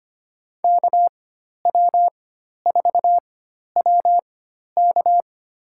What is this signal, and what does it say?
Morse code, heard as a keyed tone, about 25 wpm, spaced as for 12 wpm, 700 Hz. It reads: KW4WK
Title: USA Call Signs - 4x Character Spacing - 25wpm